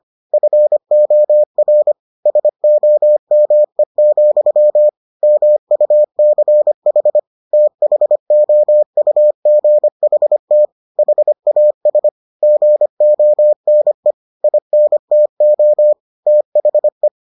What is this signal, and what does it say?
Morse code, heard as a keyed tone, about 25 wpm, 600 Hz. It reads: FOR SOME, MUCH THOUGHT HAS GONE INTO THE